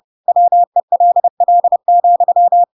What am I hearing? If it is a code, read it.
Morse code, heard as a keyed tone, about 30 wpm, 700 Hz. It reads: WELL,
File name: Pinocchio / Pinocchio ch1 30 30 ik1hge